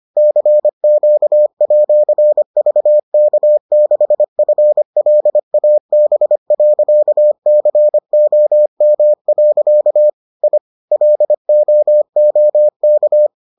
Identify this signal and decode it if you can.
Morse code, heard as a keyed tone, about 25 wpm, 600 Hz. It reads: CQ@VK6FLAB.COM. I LOOK